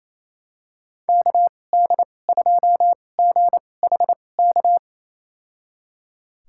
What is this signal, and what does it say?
Morse code, heard as a keyed tone, about 28 wpm, spaced as for 20 wpm, 690 Hz. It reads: KD2GHK
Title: USA Call Signs - 2x Character Spacing - 28wpm